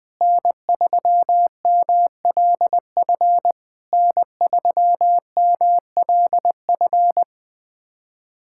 Morse code, heard as a keyed tone, about 20 wpm, 700 Hz. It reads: N3MLF N3MLF
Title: SOTA QSO Practice 20wpm